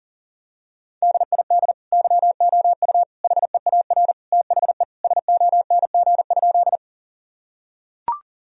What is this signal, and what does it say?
Morse code, heard as a keyed tone, about 40 wpm, 700 Hz. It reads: DID YOU HEAR THE SONG?
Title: Sentences from Top 1000 Words - Review Part 2 - 40wpm